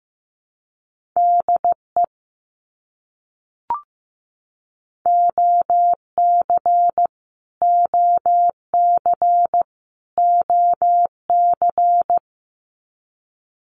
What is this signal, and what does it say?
Morse code, heard as a keyed tone, about 15 wpm, 700 Hz. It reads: DE OC OC OC
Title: Abbreviations Repeated 3x - 15wpm